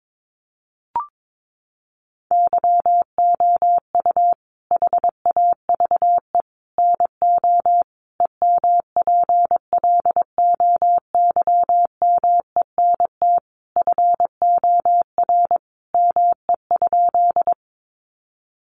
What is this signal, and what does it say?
Morse code, heard as a keyed tone, about 22 wpm, 700 Hz. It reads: YOU HAVE NO EMPLOYMENT FOR ME?
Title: Sentences from Top 1500 Words 22wpm